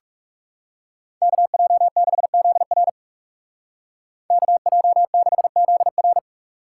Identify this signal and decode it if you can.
Morse code, heard as a keyed tone, about 39 wpm, 700 Hz. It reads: KJ6ZR KJ6ZR